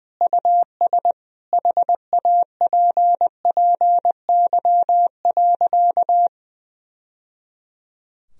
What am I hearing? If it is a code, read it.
Morse code, heard as a keyed tone, about 21 wpm, 700 Hz. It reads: US HAPPY.
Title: Sentences from Top 1000 Words - Review Part 2 - 20wpm